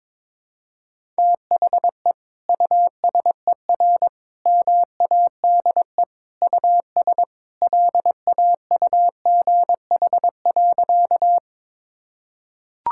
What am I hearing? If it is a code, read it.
Morse code, heard as a keyed tone, about 22 wpm, 700 Hz. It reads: THE USER MADE US LAUGH.
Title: Sentences from Top 1000 Words - Review Part 5 - 22wpm